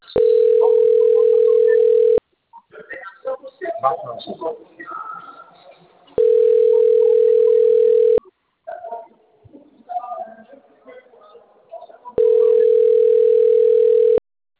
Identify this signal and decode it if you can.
Ringback tone (North America), 440 + 480 Hz, with speech.
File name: out- - - - - .16027